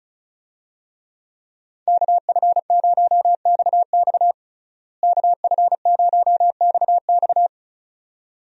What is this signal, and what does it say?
Morse code, heard as a keyed tone, about 34 wpm, 700 Hz. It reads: KF0XX KF0XX
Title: USA Call Signs - 5 Characters - Repeated 2x - 35wpm